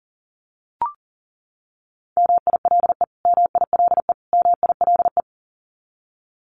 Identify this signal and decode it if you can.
Morse code, heard as a keyed tone, about 40 wpm, 700 Hz. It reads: MILE MILE MILE
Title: Top 500 Words Repeated 40wpm